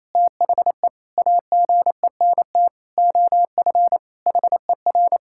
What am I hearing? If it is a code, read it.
Morse code, heard as a keyed tone, about 28 wpm, 700 Hz. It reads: THE AGENT OF HER
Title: Sentences from Top 1600 Words 28wpm